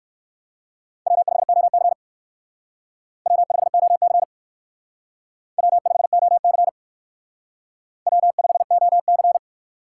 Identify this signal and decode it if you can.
Morse code, heard as a keyed tone, about 56 wpm, 700 Hz. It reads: W5OC W5OC W5OC W5OC
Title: Speed Racing - USA 1x2 Call Signs 45wpm